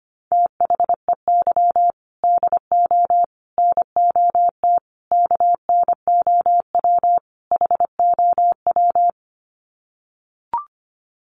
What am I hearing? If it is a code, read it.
Morse code, heard as a keyed tone, about 25 wpm, 700 Hz. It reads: THEY DO NOT KNOW HOW
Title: Sets of 5 Words Encore2 25wpm